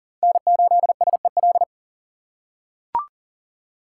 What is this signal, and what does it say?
Morse code, heard as a keyed tone, about 40 wpm, 700 Hz. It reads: N8SEL
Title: USA Call Signs - 5 Characters - 4x Character Spacing - 40wpm